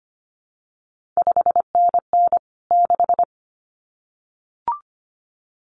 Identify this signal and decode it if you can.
Morse code, heard as a keyed tone, about 25 wpm, 700 Hz. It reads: 5NN 6
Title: CQ WW DX Contest Exchange Encore - 25wpm